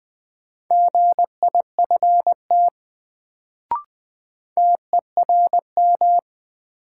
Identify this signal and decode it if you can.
Morse code, heard as a keyed tone, about 20 wpm, 700 Hz. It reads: GIFT TERM
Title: Sentences from Top 900 Words - New Word Drill - 20wpm